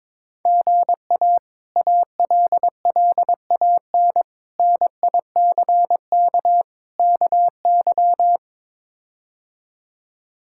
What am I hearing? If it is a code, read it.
Morse code, heard as a keyed tone, about 22 wpm, 700 Hz. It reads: GA ALLAN NICK KY
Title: K1USN SST QSO Practice - 22wpm